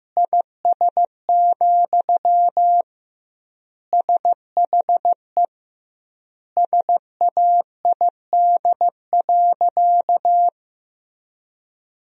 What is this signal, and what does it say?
Morse code, heard as a keyed tone, about 15 wpm, 700 Hz. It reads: IS, SHE SAID.
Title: 2x Word-Spacing - Sentences from Top 100 Words - 15wpm